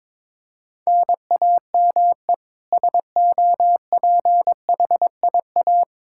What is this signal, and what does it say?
Morse code, heard as a keyed tone, about 22 wpm, 700 Hz. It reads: NAME SOPHIA
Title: Practice QSO - Exchange 1 - Beginning - 22wpm